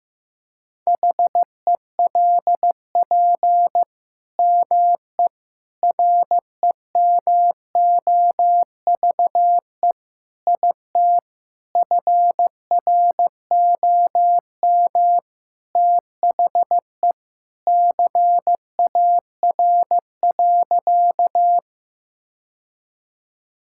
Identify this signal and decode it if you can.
Morse code, heard as a keyed tone, about 15 wpm, 700 Hz. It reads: HELP ME REMOVE IT FROM THE CAR.
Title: Sentences from Top 1000 Words - Review Part 5 - 15wpm